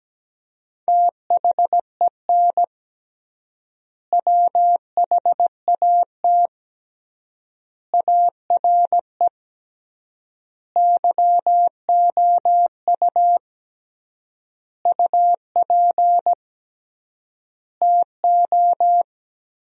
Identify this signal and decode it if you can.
Morse code, heard as a keyed tone, about 17 wpm, 700 Hz. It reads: THEN WHAT ARE YOU UP TO
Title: 3x Word-Spacing - Sentences from Top 100 Words - 17wpm